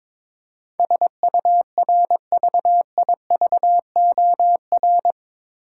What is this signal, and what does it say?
Morse code, heard as a keyed tone, about 21 wpm, 700 Hz. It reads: SURVIVOR
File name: practice / 22wpm / Eight Letter Words - 22wpm